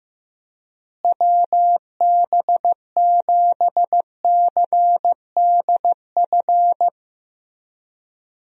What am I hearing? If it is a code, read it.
Morse code, heard as a keyed tone, about 15 wpm, 700 Hz. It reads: WB7CDF
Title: USA Call Signs - 6 Characters - Repeated 3x - 15wpm